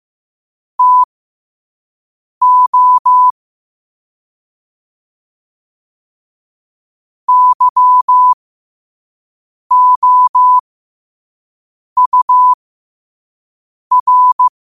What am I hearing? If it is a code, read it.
Morse code, heard as a keyed tone, about 15 wpm, spaced as for 5 wpm, 1000 Hz. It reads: TO YOUR